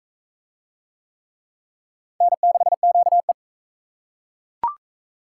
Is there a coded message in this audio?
Yes: Morse code from 2.2 to 3.3 s, heard as a keyed tone, about 37 wpm, 700 Hz. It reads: N6QE